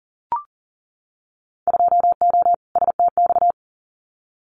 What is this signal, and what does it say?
Morse code, heard as a keyed tone, about 39 wpm, 700 Hz. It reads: 2O STX
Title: Winter Field Day Exchanges 40wpm